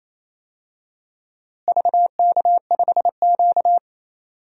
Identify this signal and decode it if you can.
Morse code, heard as a keyed tone, about 28 wpm, 700 Hz. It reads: VK5Q
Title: DXCC-100 Call Signs - 4 Characters - Repeated 2x - 28wpm